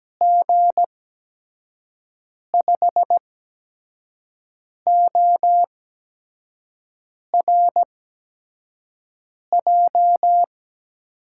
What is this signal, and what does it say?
Morse code, heard as a keyed tone, about 17 wpm, spaced as for 5 wpm, 700 Hz. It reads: G5ORJ
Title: USA Call Signs - 6 Characters - 8x Character Spacing - 17wpm